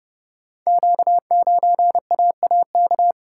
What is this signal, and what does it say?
Morse code, heard as a keyed tone, about 30 wpm, 700 Hz. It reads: Q9AAK
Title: Code Groups - 5 Letter:Numbers - 30wpm